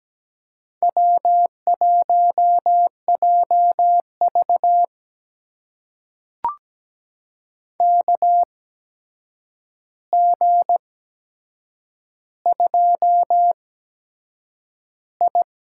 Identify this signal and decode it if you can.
Morse code, heard as a keyed tone, about 17 wpm, spaced as for 7 wpm, 700 Hz. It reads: W1JV KG2I
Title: USA Call Signs - 4 Characters - 8x Character Spacing - 17wpm